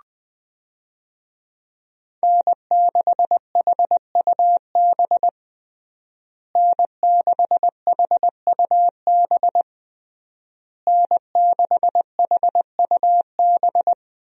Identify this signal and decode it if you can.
Morse code, heard as a keyed tone, about 20 wpm, 700 Hz. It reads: N6HUB N6HUB N6HUB
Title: USA Call Signs - 5 Characters - Repeated 3x - 20wpm